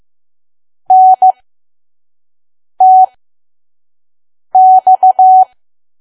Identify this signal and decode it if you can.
Morse code, heard as a keyed tone, about 15 wpm, spaced as for 5 wpm, 750 Hz. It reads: NTX
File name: Archive / 5 WPM files / PM